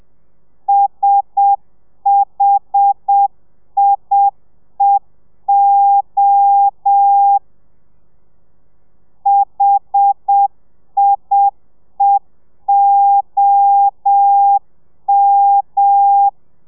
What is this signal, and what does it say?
Morse code, heard as a keyed tone, about 7 wpm, 800 Hz. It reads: SHIEO HIEOM
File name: telegrafia / i2tfj / Lez1 30